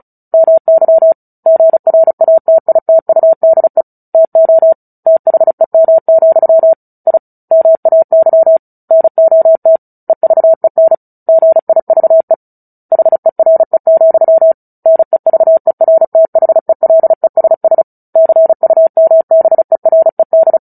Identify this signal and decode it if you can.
Morse code, heard as a keyed tone, about 34 wpm, 650 Hz. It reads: MY GRATITUDE TO THEM, I MAY NOT EVEN GIVE HERE, NEVERTHELESS CUMBERED
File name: Book Two / 09-Chapter 9 - 35 WPM